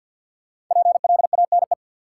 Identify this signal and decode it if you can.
Morse code, heard as a keyed tone, about 46 wpm, 700 Hz. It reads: PLANE
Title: Sentences from Top 1000 Words - Review Part 3 - New Word Drill - 50wpm